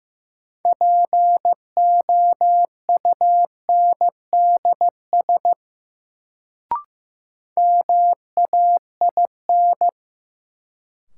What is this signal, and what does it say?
Morse code, heard as a keyed tone, about 15 wpm, 700 Hz. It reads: POUNDS MAIN
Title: Sentences from Top 1000 Words - Review Part 2 - New Word Drill - 15wpm